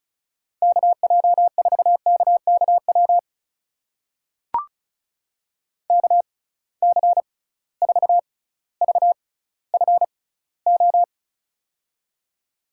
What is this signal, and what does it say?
Morse code, heard as a keyed tone, about 36 wpm, spaced as for 18 wpm, 700 Hz. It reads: KJ4KKW KC4VFO
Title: USA Call Signs - 6 Characters - 6x Character Spacing - 35wpm